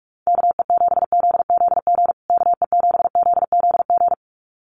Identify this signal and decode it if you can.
Morse code, heard as a keyed tone, about 44 wpm, 700 Hz. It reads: KE7ZZG KE7ZZG
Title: USA Call Signs Repeated 2x 45wpm